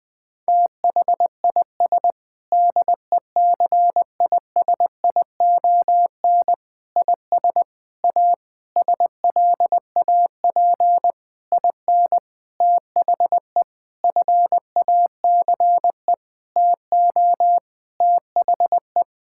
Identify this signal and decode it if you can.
Morse code, heard as a keyed tone, about 21 wpm, 700 Hz. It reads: THIS DECISION IS A SLAP IN THE FACE TO THE
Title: Sentences with 8-Word Phrases - 20wpm